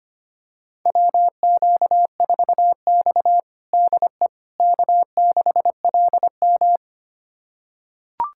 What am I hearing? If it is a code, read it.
Morse code, heard as a keyed tone, about 25 wpm, 700 Hz. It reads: WQ4X DE K6LM
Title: USA Callsign DE Callsign - 4 Characters - 25wpm